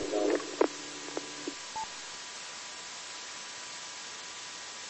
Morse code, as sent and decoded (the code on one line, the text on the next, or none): none